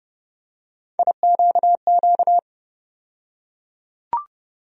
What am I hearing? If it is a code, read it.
Morse code, heard as a keyed tone, about 30 wpm, 700 Hz. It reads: IQQ
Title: USA Call Sign Postfixes - 3 Letters 30wpm